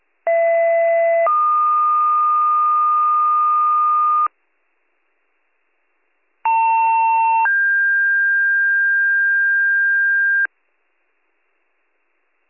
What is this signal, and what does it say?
Two-tone page, heard as a one-second tone, then a three-second tone, about 670 then 1220 Hz, about 900 then 1600 Hz. It is heard from 0.3 to 10.5 s.